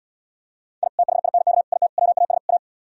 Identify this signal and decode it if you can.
Morse code, heard as a keyed tone, about 79 wpm, 700 Hz. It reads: I THINK IT WAS A